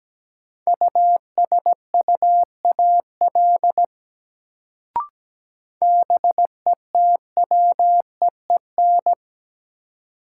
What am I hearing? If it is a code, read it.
Morse code, heard as a keyed tone, about 17 wpm, 700 Hz. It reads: USUAL BETWEEN
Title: Sentences from Top 400 Words - New Word Drill - 17wpm